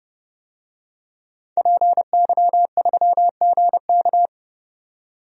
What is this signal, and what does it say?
Morse code, heard as a keyed tone, about 32 wpm, 700 Hz. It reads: PY3GK
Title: DXCC-100 Call Signs - 5 Characters - 30wpm